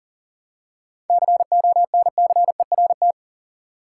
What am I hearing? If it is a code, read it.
Morse code, heard as a keyed tone, about 40 wpm, 700 Hz. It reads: CONCERT
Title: Seven Letter Words - 40wpm